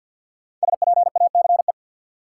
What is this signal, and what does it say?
Morse code, heard as a keyed tone, about 52 wpm, 700 Hz. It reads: SPACE
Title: Sentences from Top 1000 Words - Review Part 5 - New Word Drill - 50wpm